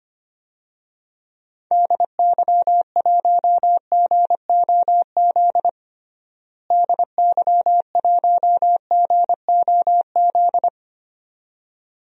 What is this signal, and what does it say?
Morse code, heard as a keyed tone, about 25 wpm, 700 Hz. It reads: DY1GOZ DY1GOZ
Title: DXCC-100 Call Signs - 6 Characters - Repeated 2x - 25wpm